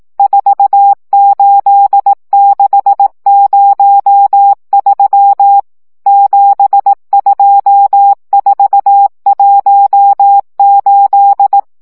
Morse code, heard as a keyed tone, about 18 wpm, 800 Hz. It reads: 48603 72418
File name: morze vizsga 2011-1 15